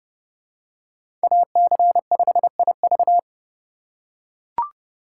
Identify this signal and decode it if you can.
Morse code, heard as a keyed tone, about 30 wpm, 700 Hz. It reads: AC5IV
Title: USA Call Signs - 3x Character Spacing - 30wpm